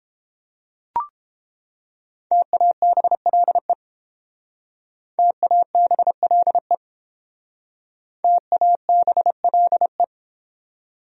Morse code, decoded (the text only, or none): TABLE TABLE TABLE